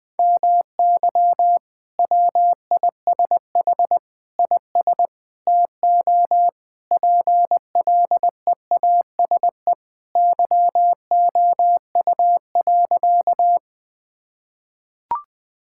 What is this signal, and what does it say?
Morse code, heard as a keyed tone, about 20 wpm, 700 Hz. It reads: MY WISH IS TO PLEASE YOU.
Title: Sentences from Top 1600 Words 20wpm